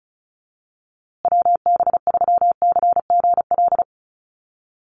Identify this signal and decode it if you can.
Morse code, heard as a keyed tone, about 34 wpm, 700 Hz. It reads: WB3CGL
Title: USA Call Signs Repeated 2x 35wpm